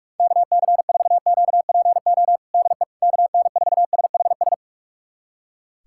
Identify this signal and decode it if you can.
Morse code, heard as a keyed tone, about 48 wpm, 700 Hz. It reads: KC4QPO DE KN4SHS
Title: USA Callsign DE Callsign 45wpm